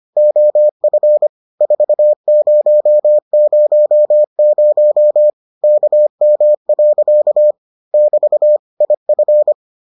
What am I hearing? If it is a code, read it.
Morse code, heard as a keyed tone, about 25 wpm, 600 Hz. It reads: OF 4000 KM. <BT> IF